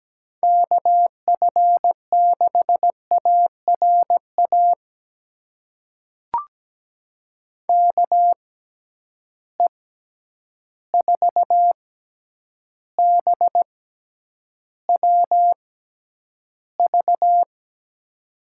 Morse code, heard as a keyed tone, about 17 wpm, spaced as for 9 wpm, 700 Hz. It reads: KF6ARA KE4BWV
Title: USA Call Signs - 6 Characters - 6x Character Spacing - 17wpm